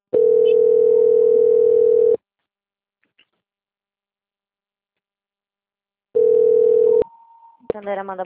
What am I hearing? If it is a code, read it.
Ringback tone (North America), 440 + 480 Hz, with speech.
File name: out- - - - - .5881